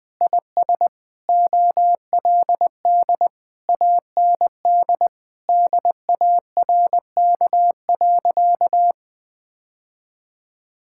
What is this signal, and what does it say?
Morse code, heard as a keyed tone, about 20 wpm, 700 Hz. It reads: IS OLD AND DARK.
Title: Sentences from Top 1000 Words - Review Part 5 - 20wpm